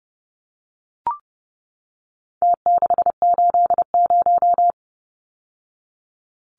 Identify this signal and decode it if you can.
Morse code, heard as a keyed tone, about 30 wpm, 700 Hz. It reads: T680
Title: Numbers - Contest Style - 4-Digit 30wpm